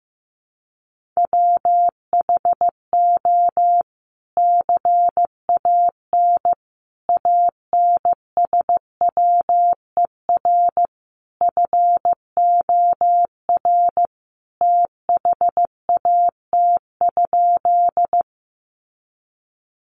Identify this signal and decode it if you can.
Morse code, heard as a keyed tone, about 15 wpm, 700 Hz. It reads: WHO CAN ANSWER FOR THAT?
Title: Sentences from Top 300 Words 15wpm